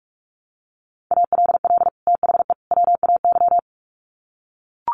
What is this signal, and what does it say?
Morse code, heard as a keyed tone, about 44 wpm, 700 Hz. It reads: ALL THE WAY
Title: Sets of 3 Words 45wpm